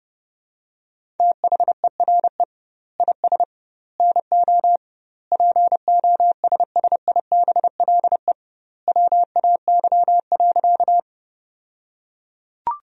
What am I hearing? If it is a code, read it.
Morse code, heard as a keyed tone, about 32 wpm, 700 Hz. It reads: THERE IS NO POSSIBLE WAY.
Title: 2x Word-Spacing - Sentences from Top 500 Words - 30wpm